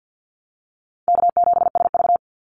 Morse code, heard as a keyed tone, about 50 wpm, 700 Hz. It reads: K7SV
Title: Easy Call Signs Repeated 2x - 50wpm